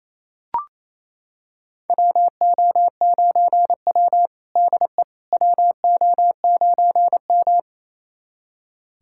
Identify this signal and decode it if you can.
Morse code, heard as a keyed tone, about 28 wpm, 700 Hz. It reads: WO9W DE WO9M
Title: USA Callsign DE Callsign - 4 Characters - 28wpm